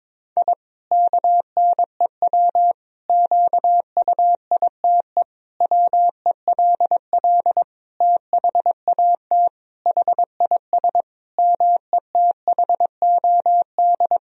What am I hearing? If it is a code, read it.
Morse code, heard as a keyed tone, about 22 wpm, 700 Hz. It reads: I KNEW QUITE WELL THAT HIS METHOD
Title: Sentences from Top 1000 Words - Review Part 3 - 22wpm